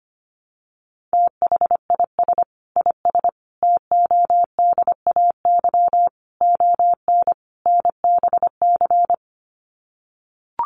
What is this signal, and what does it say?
Morse code, heard as a keyed tone, about 25 wpm, 700 Hz. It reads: THIS IS TODAY ON NBC
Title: Sets of 5 Words Encore 25wpm